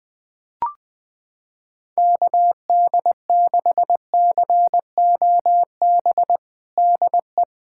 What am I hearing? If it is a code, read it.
Morse code, heard as a keyed tone, about 21 wpm, 700 Hz. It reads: KD6COB DE
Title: USA Callsign DE Callsign - 6 Characters - 20wpm